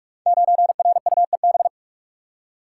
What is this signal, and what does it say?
Morse code, heard as a keyed tone, about 44 wpm, 700 Hz. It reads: 9RUEB
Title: Code Groups - 5 Letter:Numbers - 45wpm